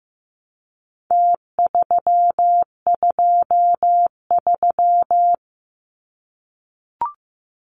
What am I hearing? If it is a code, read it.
Morse code, heard as a keyed tone, about 15 wpm, 700 Hz. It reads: T323
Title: Numbers - Contest Style - 4-Digit 15wpm